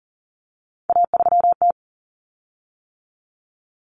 Morse code, heard as a keyed tone, about 40 wpm, 700 Hz. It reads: A3T
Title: Common Cut Numbers - 3-Digit 40wpm